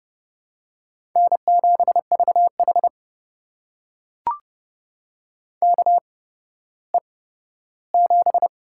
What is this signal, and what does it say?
Morse code, heard as a keyed tone, about 30 wpm, spaced as for 15 wpm, 700 Hz. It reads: N7VH KE7
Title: USA Call Signs - 8x Character Spacing - 30wpm